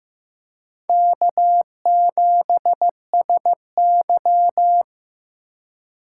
Morse code, heard as a keyed tone, about 18 wpm, 700 Hz. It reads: K7SY
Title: Speed Racing - USA 1x2 Call Signs 15wpm